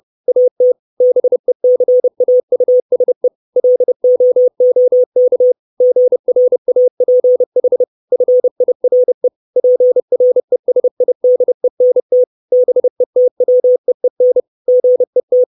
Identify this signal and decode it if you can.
Morse code, heard as a keyed tone, about 30 wpm, 500 Hz. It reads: AT BECAUSE LOOK GRAPH FIRE PRESIDENT BETWEEN GET
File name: 632 Words 30WPM